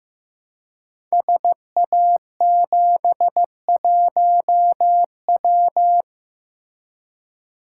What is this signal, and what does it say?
Morse code, heard as a keyed tone, about 15 wpm, 700 Hz. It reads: SA71W
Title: Code Groups - 5 Letter:Numbers - 15wpm